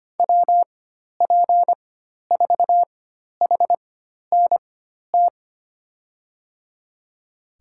Morse code, heard as a keyed tone, about 25 wpm, spaced as for 12 wpm, 700 Hz. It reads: WP4HNT
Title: USA Call Signs - 6 Characters - 4x Character Spacing - 25wpm